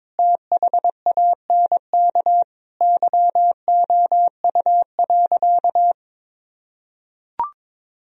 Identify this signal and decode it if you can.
Morse code, heard as a keyed tone, about 22 wpm, 700 Hz. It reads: THANK YOU.
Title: Sentences from Top 1000 Words - Review Part 1 - 22wpm